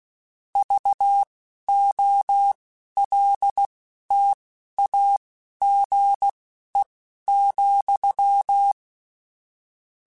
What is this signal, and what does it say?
Morse code, heard as a keyed tone, about 16 wpm, spaced as for 11 wpm, 800 Hz. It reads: VOLTAGE,
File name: Technical Subjects / Technical Subjects-mp3s / Sw Pwr Sup @10